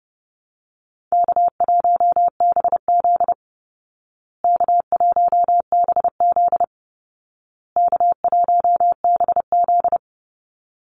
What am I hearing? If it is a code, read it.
Morse code, heard as a keyed tone, about 30 wpm, 700 Hz. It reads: K1BZ K1BZ K1BZ
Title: Call Signs Repeated 2x - 30wpm